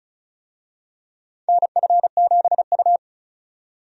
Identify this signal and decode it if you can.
Morse code, heard as a keyed tone, about 34 wpm, 700 Hz. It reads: NF7U
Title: USA Call Signs - 4 Characters - 35wpm